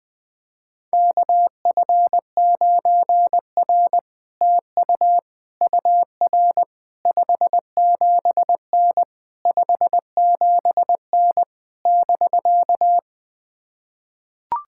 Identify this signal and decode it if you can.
Morse code, heard as a keyed tone, about 20 wpm, 700 Hz. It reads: KF9R TU UR 57N 57N <BK>
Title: POTA QSO Practice - 20wpm